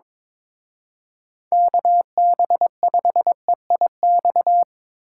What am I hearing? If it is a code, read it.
Morse code, heard as a keyed tone, about 26 wpm, 700 Hz. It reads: KB5EIX
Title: Speed Racing - USA 2x3 Call Signs 22wpm